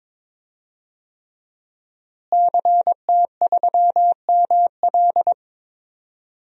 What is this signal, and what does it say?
Morse code, heard as a keyed tone, about 22 wpm, 700 Hz. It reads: CT3ML
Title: DXCC-100 Call Signs - 5 Characters - Repeated 3x - 22wpm